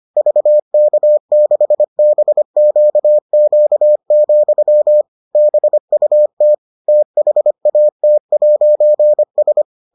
Morse code, heard as a keyed tone, about 25 wpm, 600 Hz. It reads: VK6BQQ, BUT THAT'S